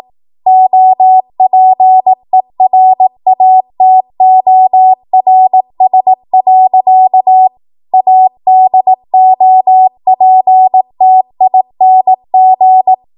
Morse code, heard as a keyed tone, about 18 wpm, 750 Hz. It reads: OPERATORS. ADOPTING
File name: Morse / 18WPM